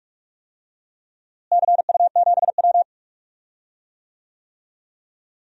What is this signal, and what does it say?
Morse code, heard as a keyed tone, about 46 wpm, 700 Hz. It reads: CU7W
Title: DXCC-100 Call Signs - 4 Characters - Repeated 3x - 45wpm